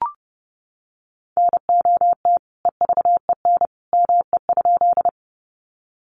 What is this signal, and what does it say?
Morse code, heard as a keyed tone, about 30 wpm, 700 Hz. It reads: NOT EVEN ME?
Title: Sentences from Top 600 Words 30wpm